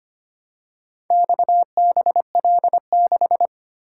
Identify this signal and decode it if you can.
Morse code, heard as a keyed tone, about 25 wpm, 700 Hz. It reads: XBL6
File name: practice / 25wpm / Four Letter-Number Combinations - 25wpm